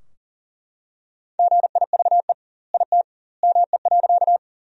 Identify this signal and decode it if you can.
Morse code, heard as a keyed tone, about 40 wpm, 700 Hz. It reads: GIVE IT ME.
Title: 2x Word-Spacing - Sentences from Top 200 Words - 40wpm